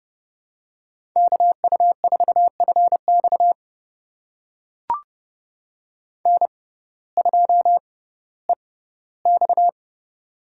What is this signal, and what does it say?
Morse code, heard as a keyed tone, about 31 wpm, spaced as for 17 wpm, 700 Hz. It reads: KU4FX N2EX
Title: USA Call Signs - 6x Character Spacing - 30wpm